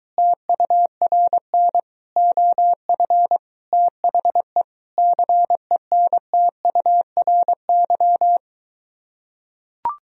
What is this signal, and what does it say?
Morse code, heard as a keyed tone, about 23 wpm, 700 Hz. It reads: TURN OF THE CENTURY